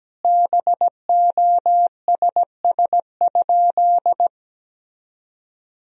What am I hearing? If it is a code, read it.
Morse code, heard as a keyed tone, about 17 wpm, 700 Hz. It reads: BOSS?
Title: Sentences from Top 1000 Words - Review Part 4 - 17wpm